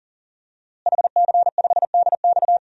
Morse code, heard as a keyed tone, about 40 wpm, 700 Hz. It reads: HC5DX
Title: DXCC-100 Call Signs - 5 Characters - Repeated 3x - 40wpm